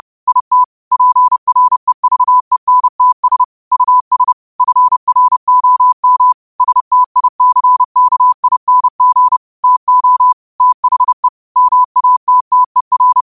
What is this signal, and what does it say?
Morse code, heard as a keyed tone, about 30 wpm, 1000 Hz. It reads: IT PREVENTS US FROM STICKING TO THE MATTER
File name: qotdfiles / 30wpm